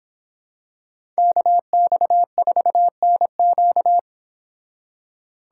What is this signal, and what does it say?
Morse code, heard as a keyed tone, about 26 wpm, 700 Hz. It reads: KX4NQ